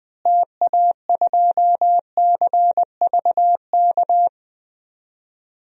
Morse code, heard as a keyed tone, about 20 wpm, 700 Hz. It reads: TA2CVK
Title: DXCC-100 Call Signs - 6 Characters - Repeated 3x - 20wpm